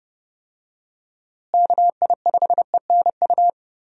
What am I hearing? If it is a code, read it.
Morse code, heard as a keyed tone, about 30 wpm, 700 Hz. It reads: KI5ENU
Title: USA Call Signs - 6 Characters - Repeated 2x - 30wpm